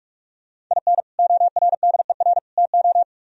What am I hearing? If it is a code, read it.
Morse code, heard as a keyed tone, about 44 wpm, 700 Hz. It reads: IN ORDER TO